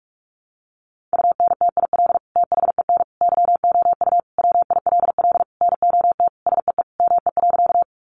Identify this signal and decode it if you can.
Morse code, heard as a keyed tone, about 44 wpm, 700 Hz. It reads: UNTIL THEN YOU WILL NOT SEE ME.
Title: Sentences from Top 600 Words 45wpm